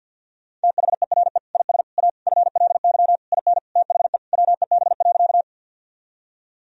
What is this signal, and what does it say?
Morse code, heard as a keyed tone, about 50 wpm, 700 Hz. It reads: THERE IS A FLY IN THE WEB.
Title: Sentences from Top 1000 Words - Review Part 4 - 50wpm